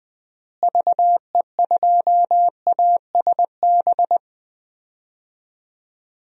Morse code, heard as a keyed tone, about 21 wpm, 700 Hz. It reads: VE2ASB
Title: DXCC-100 Call Signs - 6 Characters - Repeated 3x - 20wpm